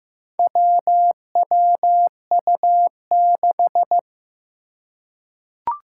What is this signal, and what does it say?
Morse code, heard as a keyed tone, about 15 wpm, 700 Hz. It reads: WWU6
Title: Four Challenging Letter-Number Combinations 15wpm